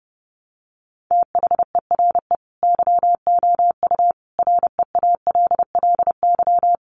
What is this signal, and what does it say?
Morse code, heard as a keyed tone, about 30 wpm, 700 Hz. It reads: THERE YOU REALLY
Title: Sentences from Top 900 Words 30wpm